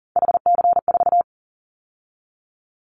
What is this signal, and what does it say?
Morse code, heard as a keyed tone, about 40 wpm, 700 Hz. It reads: HC4
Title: Callsign Prefixes for Top 100 DXCC - 40wpm